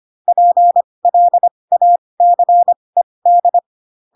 Morse code, heard as a keyed tone, about 25 wpm, spaced as for 20 wpm, 700 Hz. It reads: PLACED
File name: Pinocchio / Pinocchio ch7 15 25 ik1hge